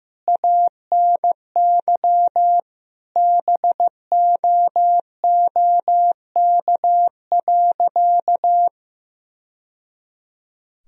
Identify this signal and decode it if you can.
Morse code, heard as a keyed tone, about 15 wpm, 700 Hz. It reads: ANY BOOK.
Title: Sentences from Top 1000 Words - Review Part 1 - 15wpm